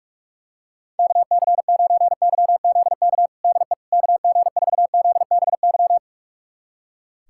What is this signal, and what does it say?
Morse code, heard as a keyed tone, about 42 wpm, 700 Hz. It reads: KC9YZK DE KG4ZBY
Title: USA Callsign DE Callsign 45wpm